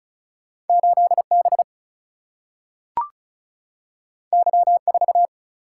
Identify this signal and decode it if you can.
Morse code, heard as a keyed tone, about 36 wpm, 700 Hz. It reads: 8B Y4
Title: Single-Letter Single-Number 35wpm